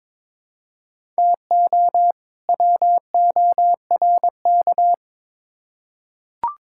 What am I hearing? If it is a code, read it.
Morse code, heard as a keyed tone, about 21 wpm, 700 Hz. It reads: TO WORK
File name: Build A Phrase - 2-Word Phrases - 22wpm